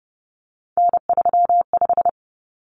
Morse code, heard as a keyed tone, about 30 wpm, 700 Hz. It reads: N35
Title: Common Cut Numbers - 3-Digit 30wpm